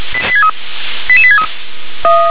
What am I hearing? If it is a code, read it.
Five-tone selective call (ZVEI), heard as quick runs of five tones, tones of about 70 ms. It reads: 88863
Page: SDR/ZVEI